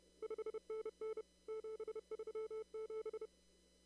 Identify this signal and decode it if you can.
Morse code, heard as a keyed tone, about 30 wpm, 440 Hz. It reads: 5NN 737
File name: CW / kt5j wpx cw-op-k5pi